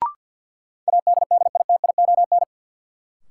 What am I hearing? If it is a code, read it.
Morse code, heard as a keyed tone, about 46 wpm, 700 Hz. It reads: ADDITION
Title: Sets of 3 Words - New Word Drill - 50wpm